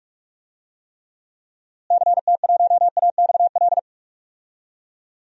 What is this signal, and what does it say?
Morse code, heard as a keyed tone, about 41 wpm, 700 Hz. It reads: CT1AXL
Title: DXCC-100 Call Signs - 6 Characters - Repeated 3x - 45wpm